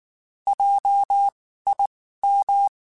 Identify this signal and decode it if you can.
Morse code, heard as a keyed tone, about 19 wpm, spaced as for 14 wpm, 800 Hz. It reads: JIM
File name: QSOs / QSOs-mp3s / WD0BGP 1@18